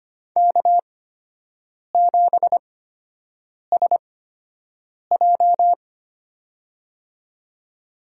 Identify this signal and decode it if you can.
Morse code, heard as a keyed tone, about 25 wpm, spaced as for 7 wpm, 700 Hz. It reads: K7SJ
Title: USA Call Signs - 4 Characters - 8x Character Spacing - 25wpm